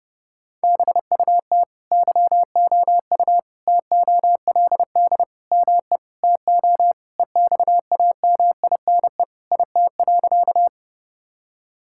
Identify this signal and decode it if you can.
Morse code, heard as a keyed tone, about 30 wpm, 700 Hz. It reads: BUT YOU TOLD ME TO EXAMINE IT.
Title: Sentences from Top 1600 Words 30wpm